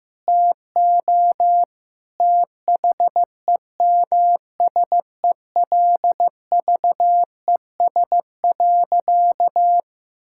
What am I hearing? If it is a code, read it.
Morse code, heard as a keyed tone, about 15 wpm, 700 Hz. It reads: TO THEMSELVES.
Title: Sentences from Top 1000 Words - Review Part 5 - 15wpm